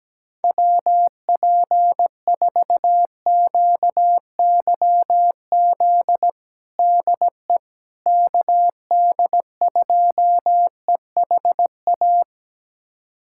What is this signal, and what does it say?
Morse code, heard as a keyed tone, about 17 wpm, 700 Hz. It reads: WP4QYZ DE KD2EHA
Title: USA Callsign DE Callsign - 6 Characters - 17wpm